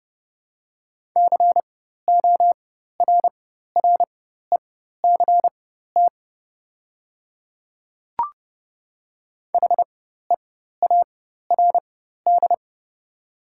Morse code, heard as a keyed tone, about 30 wpm, spaced as for 14 wpm, 700 Hz. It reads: CORRECT HEARD
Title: 4x Character Spacing - Top 500 Words - 30wpm